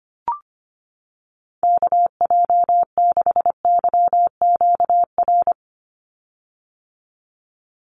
Morse code, heard as a keyed tone, about 25 wpm, 700 Hz. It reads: KJ6YQR
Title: USA Call Signs 25wpm